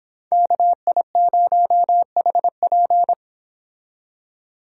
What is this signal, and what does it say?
Morse code, heard as a keyed tone, about 29 wpm, 700 Hz. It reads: KI0HP